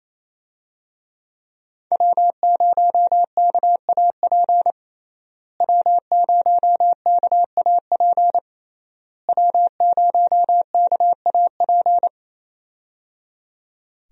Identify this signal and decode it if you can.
Morse code, heard as a keyed tone, about 27 wpm, 700 Hz. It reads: W0KAP W0KAP W0KAP
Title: USA Call Signs - 5 Characters - Repeated 3x - 28wpm